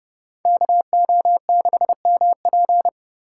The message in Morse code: KO6MP

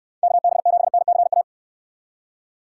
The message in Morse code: KB7NCA

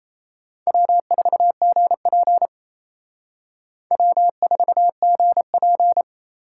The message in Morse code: W4GP W4GP

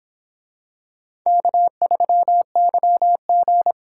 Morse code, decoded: K3YG

